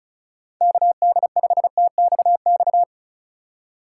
Morse code, heard as a keyed tone, about 36 wpm, 700 Hz. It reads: KD5TXX
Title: K1USN SST QSO Practice - 35wpm